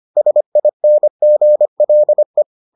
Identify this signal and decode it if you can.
Morse code, heard as a keyed tone, about 25 wpm, 600 Hz. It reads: SINGLE